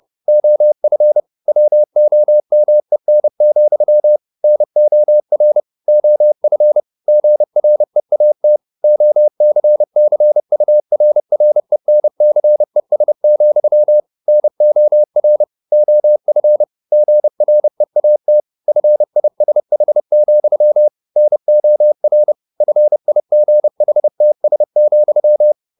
Morse code, heard as a keyed tone, about 30 wpm, 600 Hz. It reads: OF WOMEN, NOR OF GREAT OCCURRENCES, NOR OF GREAT FISH, NOR FIGHTS,
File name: the old man and the sea / the old man and the sea 1 30wpm